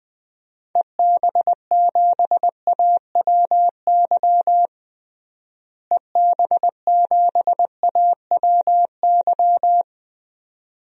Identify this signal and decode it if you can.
Morse code, heard as a keyed tone, about 20 wpm, 700 Hz. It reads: EB7AWY EB7AWY
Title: DXCC-100 Call Signs - 6 Characters - Repeated 2x - 20wpm